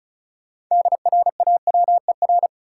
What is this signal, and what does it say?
Morse code, heard as a keyed tone, about 36 wpm, 700 Hz. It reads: DRAWER
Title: Six Letter Words - 35wpm